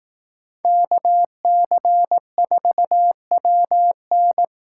Morse code, heard as a keyed tone, about 18 wpm, 700 Hz. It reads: KC4WN
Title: Speed Racing - USA Call Signs 15wpm